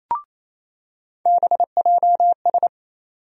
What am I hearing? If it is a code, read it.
Morse code, heard as a keyed tone, about 28 wpm, 700 Hz. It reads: BJS
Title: USA Call Sign Postfixes - 3 Letters 28wpm